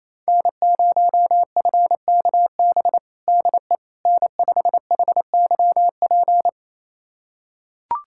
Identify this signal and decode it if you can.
Morse code, heard as a keyed tone, about 28 wpm, 700 Hz. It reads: N0FKB DE N5HYP
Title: USA Callsign DE Callsign - 5 Characters - 28wpm